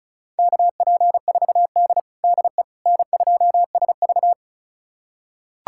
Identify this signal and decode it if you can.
Morse code, heard as a keyed tone, about 36 wpm, 700 Hz. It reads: KP4D DE N2SV
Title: USA Callsign DE Callsign - 4 Characters - 35wpm